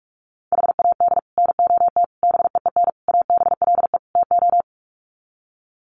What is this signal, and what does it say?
Morse code, heard as a keyed tone, about 44 wpm, 700 Hz. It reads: HAD NOT BEEN ABLE TO
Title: Sets of 5 Words Encore 45wpm